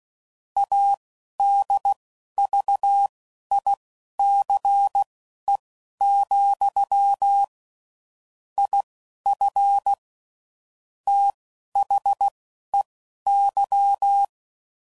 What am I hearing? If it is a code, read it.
Morse code, heard as a keyed tone, about 16 wpm, spaced as for 11 wpm, 800 Hz. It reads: ADVICE, IF THEY